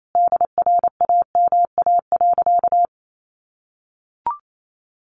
Morse code, heard as a keyed tone, about 28 wpm, 700 Hz. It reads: DRAMA.
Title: Sentences from Top 1500 Words 28wpm